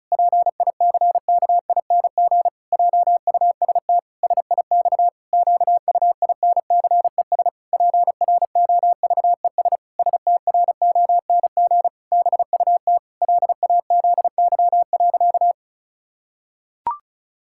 Morse code, heard as a keyed tone, about 36 wpm, 700 Hz. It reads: PICKING JUST SIX QUINCES PROVES STRONG BUT LAZY.
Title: Pangram Challenge v2 - 35wpm